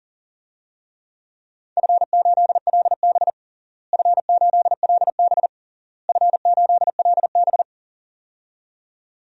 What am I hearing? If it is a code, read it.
Morse code, heard as a keyed tone, about 40 wpm, 700 Hz. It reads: F8LB F8LB F8LB
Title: DXCC-100 Call Signs - 4 Characters - Repeated 3x - 40wpm